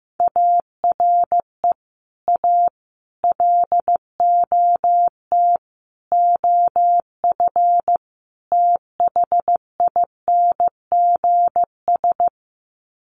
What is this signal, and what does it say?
Morse code, heard as a keyed tone, about 15 wpm, 700 Hz. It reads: ARE A LOT OF THINGS
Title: Sets of 5 Words Encore2 15wpm